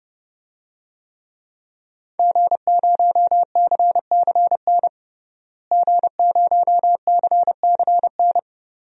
Morse code, heard as a keyed tone, about 30 wpm, 700 Hz. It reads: G0CCN G0CCN
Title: DXCC-100 Call Signs - 5 Characters - Repeated 3x - 30wpm